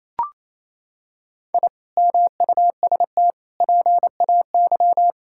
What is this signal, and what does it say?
Morse code, heard as a keyed tone, about 28 wpm, 700 Hz. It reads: I MUST PAY
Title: Sentences from Top 1000 Words - Review Part 6 - 28wpm